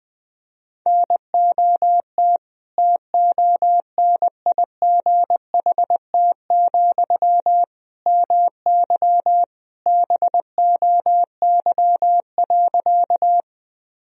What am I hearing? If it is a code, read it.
Morse code, heard as a keyed tone, about 20 wpm, 700 Hz. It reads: NOT TONIGHT, MY BOY.
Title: Sentences from Top 1000 Words - Review Part 3 - 20wpm